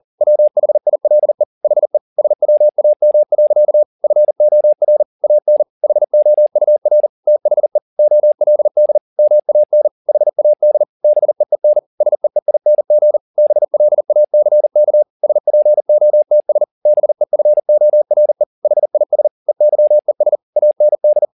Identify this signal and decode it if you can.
Morse code, heard as a keyed tone, about 40 wpm, 600 Hz. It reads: WHILE HE SWAM. FOR AN HOUR THE OLD MAN HAD BEEN SEEING BLACK SPOTS BEFORE HIS EYES AND
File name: the old man and the sea 2 40wpm